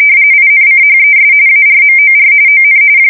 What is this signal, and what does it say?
RTTY, about 45.45 baud. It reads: INCHES ARMOR ON
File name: fsk170 / USS-New-Jersey